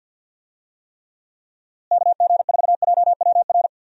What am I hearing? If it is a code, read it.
Morse code, heard as a keyed tone, about 51 wpm, 700 Hz. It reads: KG4JWR
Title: USA Call Signs - 6 Characters - Repeated 2x - 50wpm